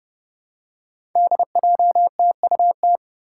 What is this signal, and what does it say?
Morse code, heard as a keyed tone, about 30 wpm, 700 Hz. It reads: DJTUT
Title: Five Challenging Letter-Number Combinations 30wpm